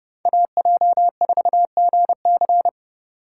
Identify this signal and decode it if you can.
Morse code, heard as a keyed tone, about 29 wpm, 700 Hz. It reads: AJ4GC